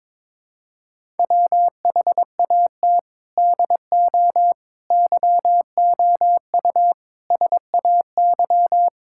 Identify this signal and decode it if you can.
Morse code, heard as a keyed tone, about 22 wpm, 700 Hz. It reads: WHAT DO YOU SAY
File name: practice / 22wpm / Build A Phrase - 5-Word Phrases - 22wpm